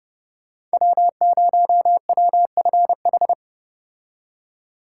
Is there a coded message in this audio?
Yes: Morse code from 0.7 to 3.3 s, heard as a keyed tone, about 34 wpm, 700 Hz. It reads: W0WFH